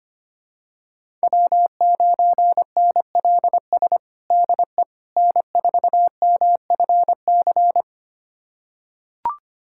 Morse code, heard as a keyed tone, about 25 wpm, 700 Hz. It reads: W9NLS DE N4MFC
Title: USA Callsign DE Callsign 25wpm